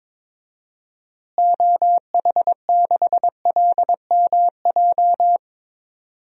Morse code, heard as a keyed tone, about 22 wpm, 700 Hz. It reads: OH6LMJ
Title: DXCC-100 Call Signs - 6 Characters - Repeated 3x - 22wpm